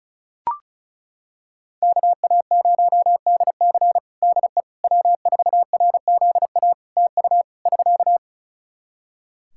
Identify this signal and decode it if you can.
Morse code, heard as a keyed tone, about 34 wpm, 700 Hz. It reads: KA0DC DE W4RZA TU <SK>
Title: Practice QSO - Exchange 4 - Ending - 35wpm